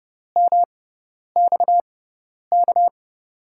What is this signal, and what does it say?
Morse code, heard as a keyed tone, about 30 wpm, spaced as for 10 wpm, 700 Hz. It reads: MXK
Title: USA Call Signs - 5 Characters - 6x Character Spacing - 30wpm